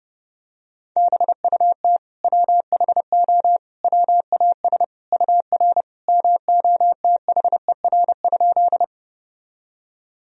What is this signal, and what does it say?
Morse code, heard as a keyed tone, about 30 wpm, 700 Hz. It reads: BUT WHO WAS UR MOTHER?
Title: Sentences from Top 200 Words and Abbreviations 30wpm